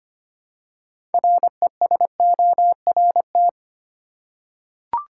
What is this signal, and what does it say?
Morse code, heard as a keyed tone, about 25 wpm, 700 Hz. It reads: RESORT
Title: Six Letter Words - 25wpm